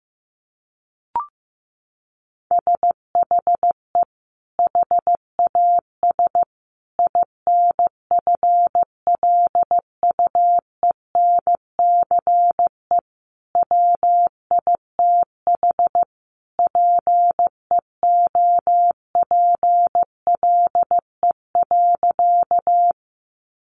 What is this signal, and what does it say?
Morse code, heard as a keyed tone, about 14 wpm, 700 Hz. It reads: SHE HAS INFLUENCE WITH PEOPLE.
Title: Sentences from Top 1100 Words 15wpm